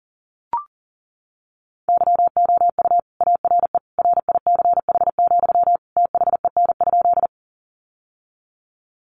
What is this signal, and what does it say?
Morse code, heard as a keyed tone, about 40 wpm, 700 Hz. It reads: YOU ARE RICH, THEN?
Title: Sentences from Top 1100 Words 40wpm